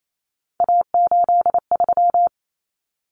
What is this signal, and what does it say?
Morse code, heard as a keyed tone, about 28 wpm, 700 Hz. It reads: A83
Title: Common Cut Numbers - 3-Digit 28wpm